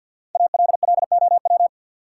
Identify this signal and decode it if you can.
Morse code, heard as a keyed tone, about 50 wpm, 700 Hz. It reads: ALLOW
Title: Sentences from Top 700 Words - New Word Drill - 50wpm